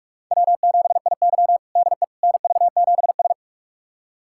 Morse code, heard as a keyed tone, about 46 wpm, 700 Hz. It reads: W7IY DE NV7S
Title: USA Callsign DE Callsign - 4 Characters - 45wpm